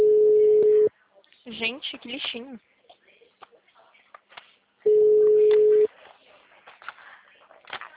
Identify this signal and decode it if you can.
Ringback tone (Europe, 425 Hz), with speech.